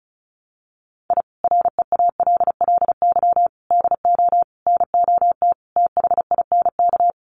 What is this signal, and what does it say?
Morse code, heard as a keyed tone, about 34 wpm, 700 Hz. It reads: I REALLY DO NOT THINK
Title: Sets of 5 Words Encore2 35wpm